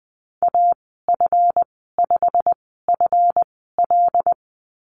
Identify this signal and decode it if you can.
Morse code, heard as a keyed tone, about 20 wpm, spaced as for 15 wpm, 700 Hz. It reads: AF5FL
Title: 140 Lesson 23g - Letter f - Call Signs 20wpm